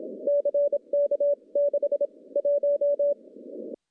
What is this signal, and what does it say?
Morse code, heard as a keyed tone, about 27 wpm, spaced as for 23 wpm, 570 Hz. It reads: CK6 1
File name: CW-SS-k5tr